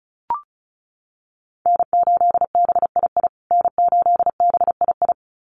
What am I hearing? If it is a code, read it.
Morse code, heard as a keyed tone, about 34 wpm, 700 Hz. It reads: N8BII N8BII
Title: USA Call Signs Repeated 2x 35wpm